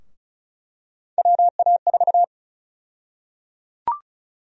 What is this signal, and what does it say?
Morse code, heard as a keyed tone, about 34 wpm, 700 Hz. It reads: WA4
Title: USA Call Sign Prefixes - 35wpm